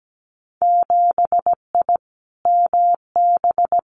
Morse code, heard as a keyed tone, about 17 wpm, 700 Hz. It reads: 7I MB
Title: Winter Field Day Exchanges Encore 17wpm